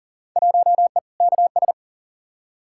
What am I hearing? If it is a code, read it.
Morse code, heard as a keyed tone, about 40 wpm, 700 Hz. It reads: 1E KS
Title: Field Day Exchanges - 40wpm